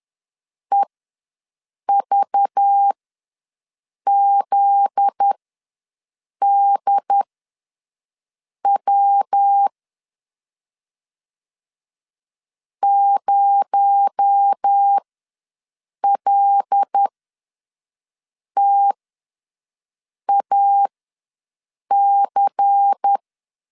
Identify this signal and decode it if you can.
Morse code, heard as a keyed tone, about 10 wpm, spaced as for 5 wpm, 800 Hz. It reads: EVZDW 0LTAC
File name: LEZIONI / 7 GF9Z / 7 GF9Z ws 10wpm 3919